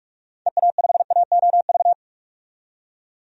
Morse code, heard as a keyed tone, about 43 wpm, 700 Hz. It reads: EA5AOV
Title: DXCC-100 Call Signs - 6 Characters - 45wpm